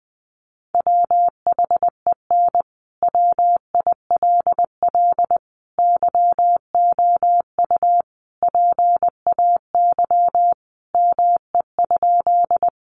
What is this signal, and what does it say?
Morse code, heard as a keyed tone, about 20 wpm, 700 Hz. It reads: WHEN WILL YOU PAY ME?
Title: Sentences from Top 700 Words 20wpm